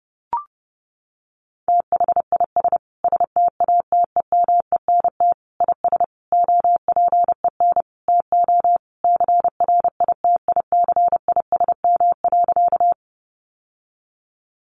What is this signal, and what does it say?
Morse code, heard as a keyed tone, about 30 wpm, 700 Hz. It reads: THIS STATEMENT IS OPEN TO CRITICISM.
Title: Sentences from Top 1300 Words 30wpm